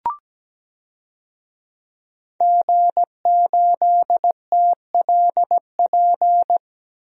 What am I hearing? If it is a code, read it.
Morse code, heard as a keyed tone, about 17 wpm, 700 Hz. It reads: G8TLP
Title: DXCC-100 Call Signs - 5 Characters - 17wpm